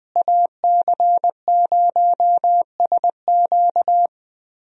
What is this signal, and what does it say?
Morse code, heard as a keyed tone, about 20 wpm, 700 Hz. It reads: AC0SQ
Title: USA Call Signs - 5 Characters - Repeated 3x - 20wpm